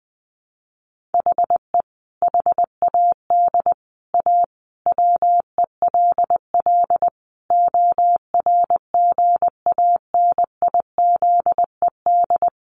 Morse code, heard as a keyed tone, about 19 wpm, 700 Hz. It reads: HE HAD A WELL ORGANIZED
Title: Sentences from Top 1300 Words 20wpm